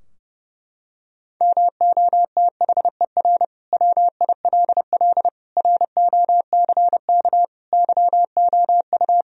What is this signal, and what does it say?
Morse code, heard as a keyed tone, about 32 wpm, 700 Hz. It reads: MOTHER WILL ROCK YOU
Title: Sentences from Top 1000 Words - Review Part 2 - 30wpm